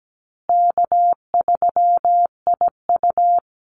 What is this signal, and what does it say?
Morse code, heard as a keyed tone, about 17 wpm, 700 Hz. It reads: K3IU
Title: Easy Call Signs Repeated 3x - 17wpm